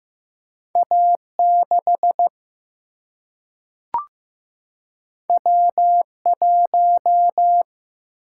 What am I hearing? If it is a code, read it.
Morse code, heard as a keyed tone, about 15 wpm, 700 Hz. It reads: A 6 W 1
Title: Single-Letter Single-Number 15wpm